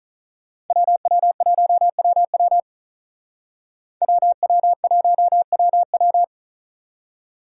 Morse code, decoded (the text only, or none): WW1WW WW1WW